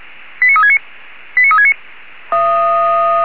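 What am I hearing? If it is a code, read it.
Five-tone selective call (ZVEI), heard as quick runs of five tones, tones of about 80 ms. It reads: 89369 89369
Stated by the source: ZVEI five-tone selective call 89360 received on SDR.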